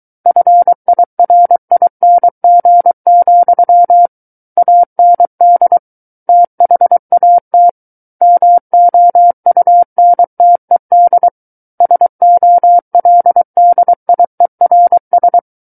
Morse code, heard as a keyed tone, about 23 wpm, 700 Hz. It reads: FIRING, AND THAT MOUNTED SOLDIERS
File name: Book One / 14-Chapter 14 - 23 WPM